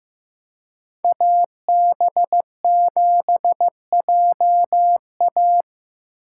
Morse code, heard as a keyed tone, about 15 wpm, 700 Hz. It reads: AB7JA
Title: USA Call Signs - 5 Characters - 4x Character Spacing - 15wpm